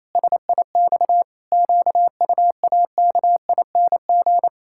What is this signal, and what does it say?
Morse code, heard as a keyed tone, about 28 wpm, 700 Hz. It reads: SIX QUAKING
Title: Pangram Challenge v2 - 28wpm